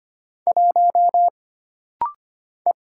Morse code, heard as a keyed tone, about 25 wpm, 700 Hz. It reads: 1 E
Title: Single Letter-Number - Rapid-Fire 25wpm